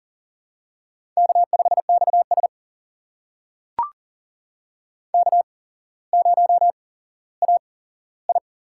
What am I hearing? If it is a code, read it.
Morse code, heard as a keyed tone, about 40 wpm, spaced as for 17 wpm, 700 Hz. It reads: K5XS K0AI
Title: USA Call Signs - 4 Characters - 8x Character Spacing - 40wpm